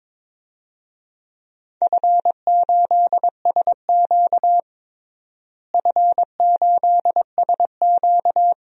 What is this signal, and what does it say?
Morse code, heard as a keyed tone, about 22 wpm, 700 Hz. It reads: F8SQ F8SQ
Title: DXCC-100 Call Signs - 4 Characters - Repeated 2x - 22wpm